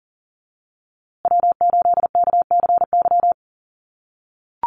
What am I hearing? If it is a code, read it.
Morse code, heard as a keyed tone, about 40 wpm, 700 Hz. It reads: W8KCY
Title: USA Call Signs Repeated 3x 40wpm